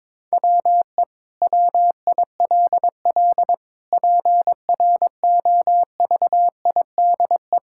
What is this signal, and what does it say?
Morse code, heard as a keyed tone, about 21 wpm, 700 Hz. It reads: WE WILL PROVIDE
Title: Sentences from Top 1000 Words - Review Part 5 - 22wpm